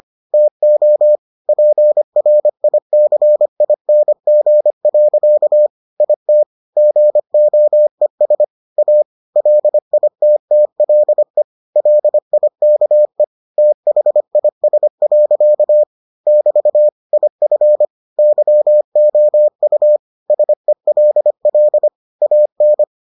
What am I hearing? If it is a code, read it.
Morse code, heard as a keyed tone, about 25 wpm, 600 Hz. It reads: TO PRICING. IT GOES A LITTLE LIKE THIS. <BT> IF YOU SELL AN